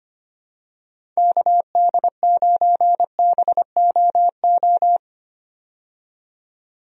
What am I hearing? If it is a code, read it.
Morse code, heard as a keyed tone, about 25 wpm, 700 Hz. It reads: KD9BOO
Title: USA Call Signs - 6 Characters - 25wpm